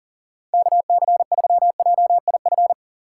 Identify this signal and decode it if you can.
Morse code, heard as a keyed tone, about 40 wpm, 700 Hz. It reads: KC3JIF
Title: USA Call Signs - 6 Characters - 6x Character Spacing - 40wpm